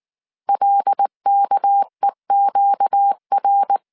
Morse code, heard as a keyed tone, about 19 wpm, 800 Hz. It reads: LKEQR
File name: CW / DET / 1 6 1500C S C 18wpm 2356